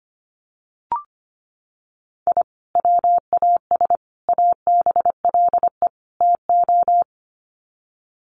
Morse code, heard as a keyed tone, about 25 wpm, 700 Hz. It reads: I WAS ABLE TO
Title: Sets of 4 Words 25wpm